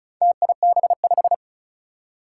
Morse code, heard as a keyed tone, about 36 wpm, 700 Hz. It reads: TIB5
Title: Four Challenging Letter-Number Combinations 35wpm